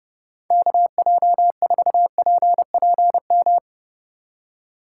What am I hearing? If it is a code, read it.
Morse code, heard as a keyed tone, about 37 wpm, 700 Hz. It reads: KJ4PPM
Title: Speed Racing - USA Call Signs 30wpm